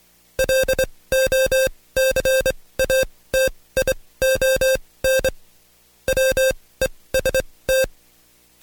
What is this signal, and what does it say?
Morse code, heard as a keyed tone, about 24 wpm, spaced as for 18 wpm, 530 Hz. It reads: LOCATION WEST
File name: Code Quick / AudioDownld2 / Seg135-Stepper8 17WPM